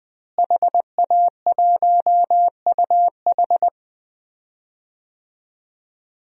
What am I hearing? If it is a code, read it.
Morse code, heard as a keyed tone, about 20 wpm, 700 Hz. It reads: HA1UH
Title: DXCC-100 Call Signs - 5 Characters - Repeated 3x - 20wpm